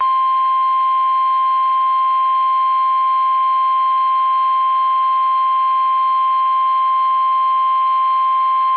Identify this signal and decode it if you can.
Emergency Alert System (SAME) tones (the 1050 Hz attention tone). It is heard from 0.7 to 8.0 s.
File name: eas / KUSO / KUSO